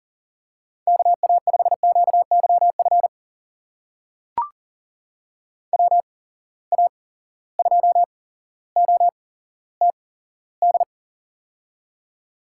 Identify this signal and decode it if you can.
Morse code, heard as a keyed tone, about 40 wpm, spaced as for 17 wpm, 700 Hz. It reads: KA5QYF WA2OTD
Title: USA Call Signs - 6 Characters - 8x Character Spacing - 40wpm